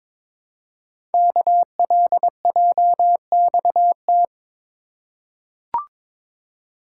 Morse code, heard as a keyed tone, about 22 wpm, 700 Hz. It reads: KLJXT
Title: Code Groups - 5 Letter:Numbers - 22wpm